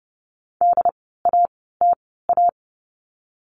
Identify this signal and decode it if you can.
Morse code, heard as a keyed tone, about 30 wpm, 700 Hz. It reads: D A T A
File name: Lesson 16f - Letter c - Words 30wpm